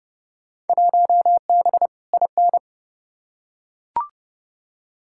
Morse code, heard as a keyed tone, about 30 wpm, 700 Hz. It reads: 1B IN
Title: Field Day Exchanges - 30wpm